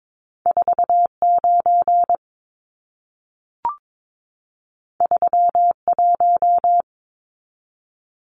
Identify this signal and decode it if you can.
Morse code, heard as a keyed tone, about 22 wpm, 700 Hz. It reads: 49 31
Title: Numbers - 2-Digit 22wpm